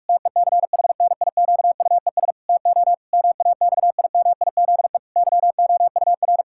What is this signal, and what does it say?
Morse code, heard as a keyed tone, about 46 wpm, 700 Hz. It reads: TECHNIQUES TO MAXIMIZE YOUR
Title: Pangram Challenge v2 - 45wpm